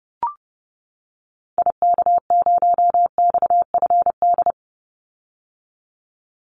Morse code, heard as a keyed tone, about 30 wpm, 700 Hz. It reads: IK0XFD
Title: Call Signs - 30wpm